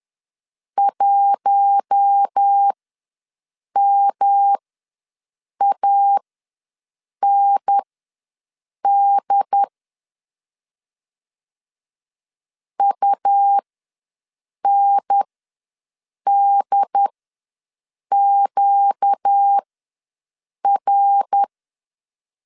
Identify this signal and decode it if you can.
Morse code, heard as a keyed tone, about 10 wpm, spaced as for 6 wpm, 800 Hz. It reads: 1MAND UNDQR
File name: LEZIONI / 5 LBVC / 5 L ws 10wpm 3636